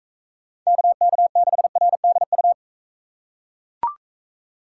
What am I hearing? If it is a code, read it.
Morse code, heard as a keyed tone, about 40 wpm, 700 Hz. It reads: KK6RDU